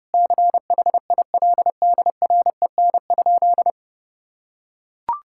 Morse code, heard as a keyed tone, about 32 wpm, 700 Hz. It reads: CHILDREN?
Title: Sentences from Top 1000 Words - Review Part 5 - 30wpm